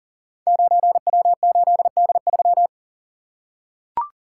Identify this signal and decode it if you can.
Morse code, heard as a keyed tone, about 41 wpm, 700 Hz. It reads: 9W8D3
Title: Code Groups - 5 Letter:Numbers - 40wpm